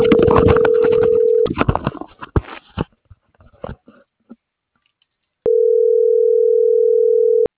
Ringback tone (North America), 440 + 480 Hz.